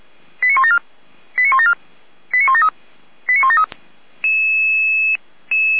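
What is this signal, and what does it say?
Five-tone selective call (ZVEI), heard as quick runs of five tones, tones of about 70 ms. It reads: 89174 89174 89173 89173
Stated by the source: ZVEI five-tone selective call 89250, received by SDR, logged by Feuerwehr Orlamünde (f85360).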